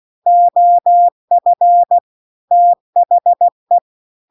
Morse code, heard as a keyed tone, about 16 wpm, 700 Hz. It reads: OF THE
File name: CW-The-Big-Twoheaded-River-2-16WPM